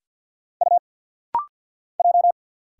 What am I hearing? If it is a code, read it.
Morse code, heard as a keyed tone, about 49 wpm, 700 Hz. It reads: U J
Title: Single Letter-Number Alt - Rapid-Fire 50wpm